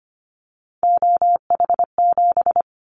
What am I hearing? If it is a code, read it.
Morse code, heard as a keyed tone, about 25 wpm, 700 Hz. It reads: OH7
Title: Callsign Prefixes for Top 25 DXCC - Repeated 3x - 25wpm